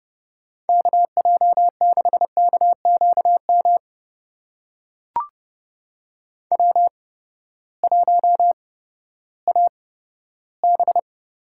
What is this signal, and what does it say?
Morse code, heard as a keyed tone, about 30 wpm, spaced as for 15 wpm, 700 Hz. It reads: KJ6KQM W1AB
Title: USA Call Signs - 8x Character Spacing - 30wpm